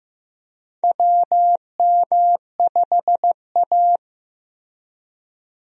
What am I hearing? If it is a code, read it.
Morse code, heard as a keyed tone, about 15 wpm, 700 Hz. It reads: WM5A
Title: USA Call Signs - 4 Characters - Repeated 3x - 15wpm